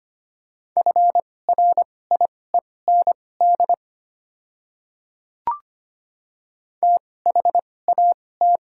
Morse code, heard as a keyed tone, about 25 wpm, spaced as for 18 wpm, 700 Hz. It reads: FRIEND THAT
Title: 2x Character Spacing - Top 300 Words - 25wpm